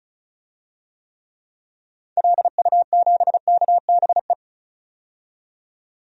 Morse code, heard as a keyed tone, about 36 wpm, 700 Hz. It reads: LU7KBE
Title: DXCC-100 Call Signs - 6 Characters - 35wpm